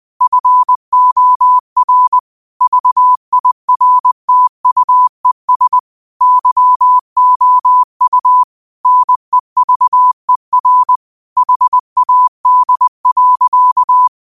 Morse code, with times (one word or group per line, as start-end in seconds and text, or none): 0.2-2.2 FOR
2.6-5.8 VIRTUES
6.2-8.4 YOU
8.9-11.0 NEVER
11.4-14.1 HAD.